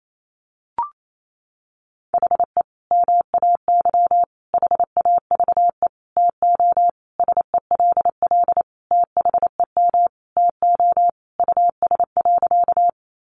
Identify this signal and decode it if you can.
Morse code, heard as a keyed tone, about 28 wpm, 700 Hz. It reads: HE MAY HAVE TO SELL THEM TO US.
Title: Sentences from Top 1100 Words 28wpm